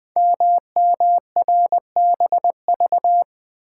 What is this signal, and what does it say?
Morse code, heard as a keyed tone, about 20 wpm, 700 Hz. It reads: MMRBV
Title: Code Groups - 5 Letters - 20wpm